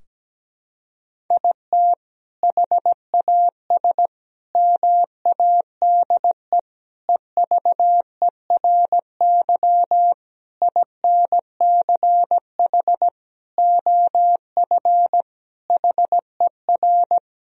Morse code, heard as a keyed tone, about 17 wpm, 700 Hz. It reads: IT HAS MADE EVERY INCH OF HER
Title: Sentences from Top 1000 Words - Review Part 3 - 17wpm